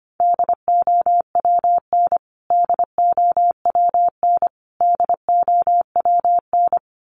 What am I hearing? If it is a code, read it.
Morse code, heard as a keyed tone, about 25 wpm, 700 Hz. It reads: DOWN DOWN DOWN
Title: Top 300 Words Repeated 25wpm